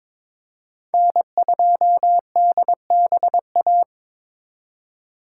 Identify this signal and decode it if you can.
Morse code, heard as a keyed tone, about 26 wpm, 700 Hz. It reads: N2DBA
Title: Speed Racing - USA 1x3 Call Signs 22wpm